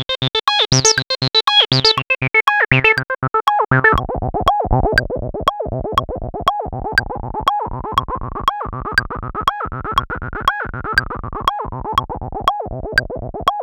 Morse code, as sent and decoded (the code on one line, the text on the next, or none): none